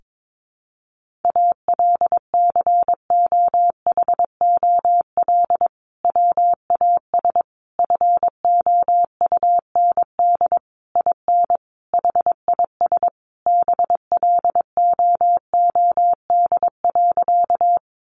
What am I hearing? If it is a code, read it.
Morse code, heard as a keyed tone, about 22 wpm, 700 Hz. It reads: ALCOHOL WAS FOUND IN HIS BLOOD.
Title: Sentences from Top 1400 Words 22wpm